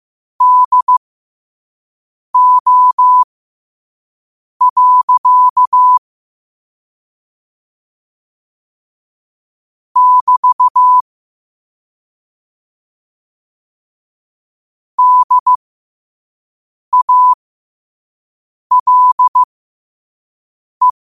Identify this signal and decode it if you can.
Morse code, heard as a keyed tone, about 15 wpm, spaced as for 5 wpm, 1000 Hz. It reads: DO. <BT> DALE
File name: qotdfiles / 5wpm